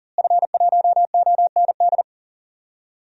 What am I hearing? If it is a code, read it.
Morse code, heard as a keyed tone, about 38 wpm, 700 Hz. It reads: F1OND
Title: DXCC-100 Call Signs - 5 Characters - Repeated 2x - 40wpm